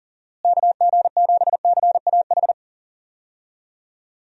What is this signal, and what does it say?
Morse code, heard as a keyed tone, about 40 wpm, 700 Hz. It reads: KG7CAH
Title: USA Call Signs - 6 Characters - Repeated 2x - 40wpm